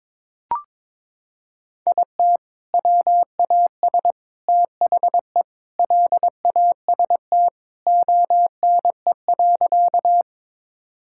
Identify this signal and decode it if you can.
Morse code, heard as a keyed tone, about 22 wpm, 700 Hz. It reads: IT WAS THE LAST ONE.
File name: practice / 22wpm / Sentences with 3-Word Phrases - 22wpm